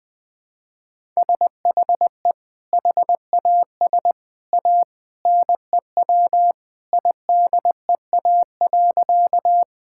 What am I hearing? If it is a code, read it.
Morse code, heard as a keyed tone, about 21 wpm, 700 Hz. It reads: SHE HAS A NEW IDEA.
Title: Sentences with 2-Word Phrases - 20wpm